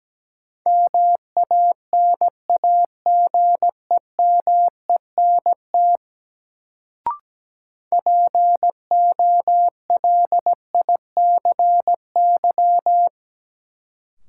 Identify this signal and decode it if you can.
Morse code, heard as a keyed tone, about 17 wpm, 700 Hz. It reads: MANAGEMENT POLICY
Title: Sentences from Top 1200 Words - New Word Drill Alt - 17wpm